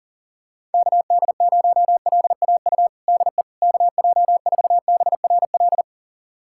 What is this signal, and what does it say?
Morse code, heard as a keyed tone, about 37 wpm, 700 Hz. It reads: KD0LAU DE KJ4BRL
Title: USA Callsign DE Callsign 40wpm